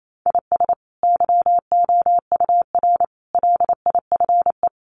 Morse code, heard as a keyed tone, about 28 wpm, 700 Hz. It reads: IS YOUR LIFE
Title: Sentences from Top 1500 Words 28wpm